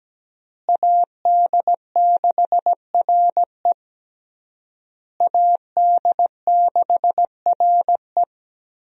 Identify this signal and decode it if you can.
Morse code, heard as a keyed tone, about 17 wpm, 700 Hz. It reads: AD6RE AD6RE
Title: USA Call Signs - 5 Characters - Repeated 3x - 17wpm